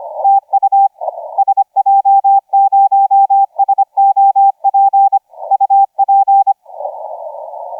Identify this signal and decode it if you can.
Morse code, heard as a keyed tone, about 25 wpm, 790 Hz, with radio noise. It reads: TU SJ0SOP UP
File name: SJ0SOP 1650UTC 15m